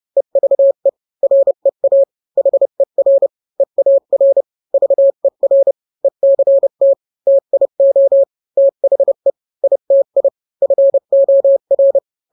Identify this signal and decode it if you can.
Morse code, heard as a keyed tone, about 30 wpm, 550 Hz. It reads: EVE REA HER EAR VER ECT TIO THE ITI FOR